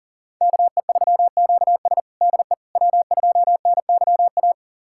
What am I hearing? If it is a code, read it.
Morse code, heard as a keyed tone, about 40 wpm, 700 Hz. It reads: KE3QS DE W2NYA
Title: USA Callsign DE Callsign - 5 Characters - 40wpm